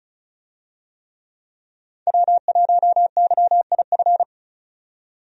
Morse code, heard as a keyed tone, about 36 wpm, 700 Hz. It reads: W1YIF
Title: USA Call Signs - 5 Characters - 35wpm